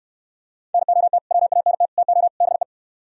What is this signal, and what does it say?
Morse code, heard as a keyed tone, about 68 wpm, 700 Hz. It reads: NOT WANT TO BE